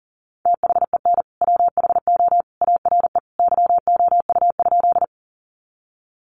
Morse code, heard as a keyed tone, about 37 wpm, 700 Hz. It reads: THEN WHO ARE YOU?
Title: Sentences from Top 100 Words 40wpm